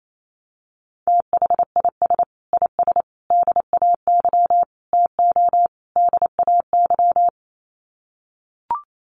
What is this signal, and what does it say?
Morse code, heard as a keyed tone, about 28 wpm, 690 Hz. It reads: THIS IS DAY TO DAY
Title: Sets of 5 Words Encore 28wpm